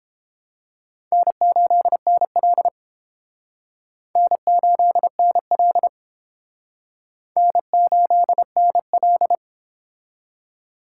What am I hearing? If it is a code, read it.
Morse code, heard as a keyed tone, about 28 wpm, 700 Hz. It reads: N8NL N8NL N8NL